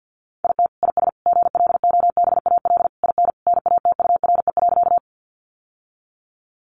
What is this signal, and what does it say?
Morse code, heard as a keyed tone, about 50 wpm, 700 Hz. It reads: IT IS GLOBAL IN NATURE.
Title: Sentences from Top 1200 Words 50wpm